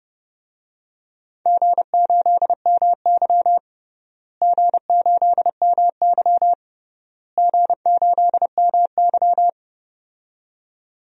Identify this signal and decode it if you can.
Morse code, heard as a keyed tone, about 30 wpm, 700 Hz. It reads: G8MY G8MY G8MY
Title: DXCC-100 Call Signs - 4 Characters - Repeated 3x - 30wpm